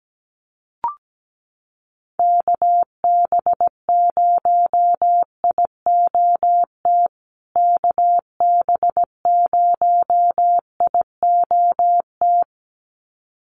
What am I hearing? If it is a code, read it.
Morse code, heard as a keyed tone, about 17 wpm, 700 Hz. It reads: KB0IOT KB0IOT
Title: USA Call Signs Repeated 2x 17wpm